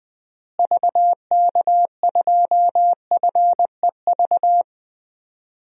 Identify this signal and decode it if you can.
Morse code, heard as a keyed tone, about 20 wpm, 700 Hz. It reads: VK2FEV
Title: DXCC-100 Call Signs - 6 Characters - Repeated 2x - 20wpm